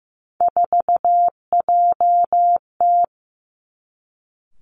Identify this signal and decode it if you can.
Morse code, heard as a keyed tone, about 15 wpm, 700 Hz. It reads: 4JT
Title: Three Challenging Letter-Number Combinations 15wpm